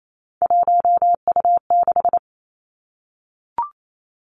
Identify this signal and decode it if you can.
Morse code, heard as a keyed tone, about 28 wpm, 700 Hz. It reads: 1U6
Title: Three Challenging Letter-Number Combinations 28wpm